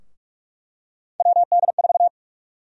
Morse code, heard as a keyed tone, about 46 wpm, 700 Hz. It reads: WD4
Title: USA Call Sign Prefixes - 45wpm